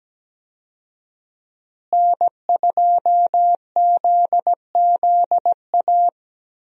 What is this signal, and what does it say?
Morse code, heard as a keyed tone, about 17 wpm, 700 Hz. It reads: N2ZZA
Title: USA Call Signs - 5 Characters - Repeated 2x - 17wpm